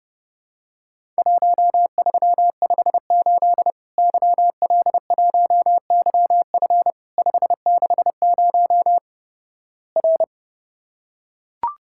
Morse code, heard as a keyed tone, about 30 wpm, 700 Hz. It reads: 1358 YL1YF 560 R
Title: Worked All Europe DX Contest - QTC Exchange - 30wpm